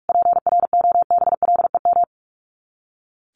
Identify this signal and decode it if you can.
Morse code, heard as a keyed tone, about 44 wpm, 700 Hz. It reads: PROBLEM
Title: Top 500 Words Repeated 45wpm